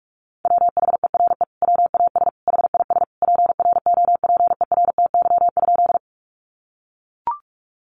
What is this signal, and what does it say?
Morse code, heard as a keyed tone, about 44 wpm, 700 Hz. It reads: WHERE WAS HIS PROPERTY?
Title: Sentences from Top 1000 Words 45wpm